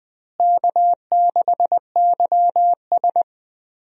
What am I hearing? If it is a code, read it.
Morse code, heard as a keyed tone, about 20 wpm, 700 Hz. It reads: K6YS
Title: USA Call Signs - 4 Characters - Repeated 2x - 20wpm